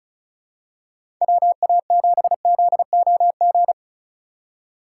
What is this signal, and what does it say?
Morse code, heard as a keyed tone, about 36 wpm, 700 Hz. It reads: WA7ZOG
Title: USA Call Signs - 3x Character Spacing - 35wpm